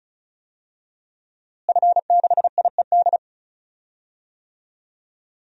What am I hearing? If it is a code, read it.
Morse code, heard as a keyed tone, about 34 wpm, 700 Hz. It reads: F6IED
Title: DXCC-100 Call Signs - 5 Characters - Repeated 2x - 35wpm